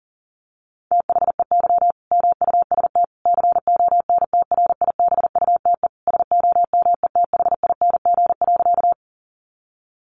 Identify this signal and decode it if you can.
Morse code, heard as a keyed tone, about 40 wpm, 700 Hz. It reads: THEY MUST CONTRIBUTE SOMETHING.
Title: Sentences from Top 1400 Words 40wpm